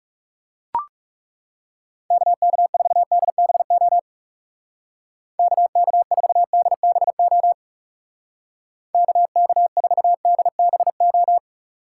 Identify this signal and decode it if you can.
Morse code, heard as a keyed tone, about 37 wpm, 700 Hz. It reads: KK4DBO KK4DBO KK4DBO
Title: Speed Racing - USA 2x3 Call Signs 30wpm